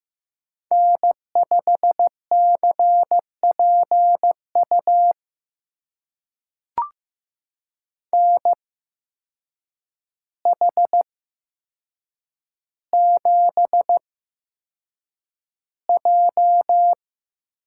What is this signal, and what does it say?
Morse code, heard as a keyed tone, about 15 wpm, spaced as for 7 wpm, 700 Hz. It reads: N5CPU NH7J
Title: USA Call Signs - 8x Character Spacing - 15wpm